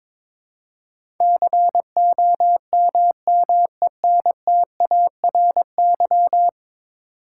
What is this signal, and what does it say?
Morse code, heard as a keyed tone, about 22 wpm, 700 Hz. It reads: COMMENTARY
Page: morse-code-ninja-practice-22wpm